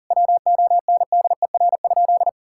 Morse code, heard as a keyed tone, about 40 wpm, 700 Hz. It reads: WONDER?
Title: Sentences from Top 1000 Words - Review Part 5 - 40wpm